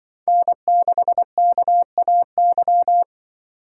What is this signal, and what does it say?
Morse code, heard as a keyed tone, about 26 wpm, 700 Hz. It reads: N6KAY